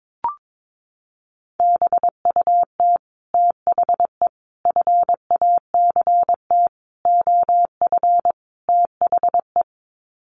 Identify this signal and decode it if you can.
Morse code, heard as a keyed tone, about 22 wpm, 700 Hz. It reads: BUT THE FACT OF THE
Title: Sets of 5 Words Encore2 22wpm